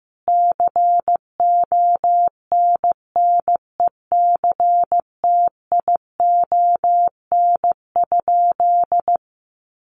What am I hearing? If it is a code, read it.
Morse code, heard as a keyed tone, about 15 wpm, 700 Hz. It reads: CONNECTION?
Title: Sentences from Top 1300 Words 15wpm